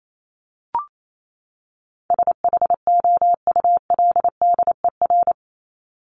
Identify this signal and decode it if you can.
Morse code, heard as a keyed tone, about 28 wpm, 700 Hz. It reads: SHOULDER
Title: Top 1501-1750 Words 28wpm